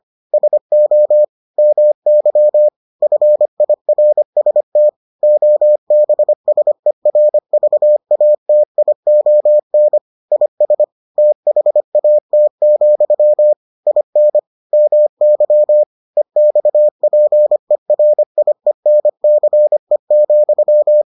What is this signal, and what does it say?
Morse code, heard as a keyed tone, about 25 wpm, 600 Hz. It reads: SO MY FIRST OBSERVATION IS THAT, IN MY EXPERIENCE,